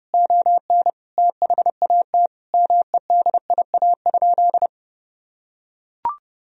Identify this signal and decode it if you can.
Morse code, heard as a keyed tone, about 30 wpm, 700 Hz. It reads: ON THAT MEDIA?
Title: Sentences from Top 1000 Words - Review Part 4 - 30wpm